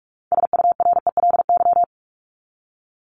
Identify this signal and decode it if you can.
Morse code, heard as a keyed tone, about 44 wpm, 700 Hz. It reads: SURELY
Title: LY Suffix - Top 100 Words with Base Words 45wpm